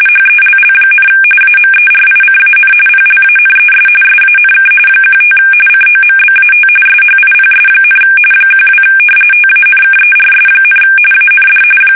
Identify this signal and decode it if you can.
RTTY, about 45.45 baud. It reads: HELL, AMERICAN PLANES SANK 15 COMBAT SHIPS, 12 TANKERS, AND
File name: fsk850 / battleshipnj